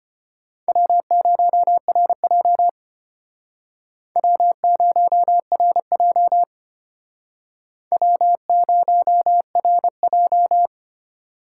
Morse code, heard as a keyed tone, about 29 wpm, 700 Hz. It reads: W0RJ W0RJ W0RJ
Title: Speed Racing - USA 1x2 Call Signs 25wpm